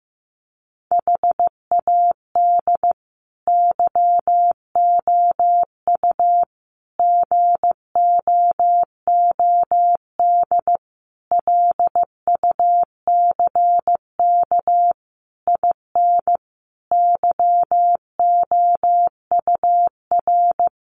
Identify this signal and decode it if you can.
Morse code, heard as a keyed tone, about 15 wpm, 700 Hz. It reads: HAD YOU GOOD LUCK IN YOUR
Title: Sentences from Top 1500 Words 15wpm